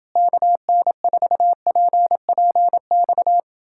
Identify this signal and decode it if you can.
Morse code, heard as a keyed tone, about 30 wpm, 700 Hz. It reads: KN4PPX